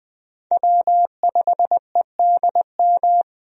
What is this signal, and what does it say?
Morse code, heard as a keyed tone, about 20 wpm, 700 Hz. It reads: W5EDM
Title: USA Call Signs - 5 Characters - 20wpm